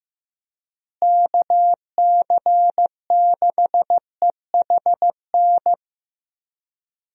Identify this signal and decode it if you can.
Morse code, heard as a keyed tone, about 15 wpm, 700 Hz. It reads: KC6EHN
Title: USA Call Signs - 6 Characters - Repeated 3x - 15wpm